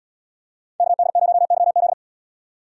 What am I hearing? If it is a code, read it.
Morse code, heard as a keyed tone, about 77 wpm, 700 Hz. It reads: KD0YZ